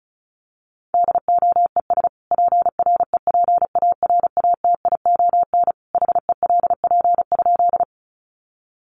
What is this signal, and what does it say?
Morse code, heard as a keyed tone, about 34 wpm, 700 Hz. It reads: DOES PREPARATION HELP?
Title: Sentences from Top 1400 Words 35wpm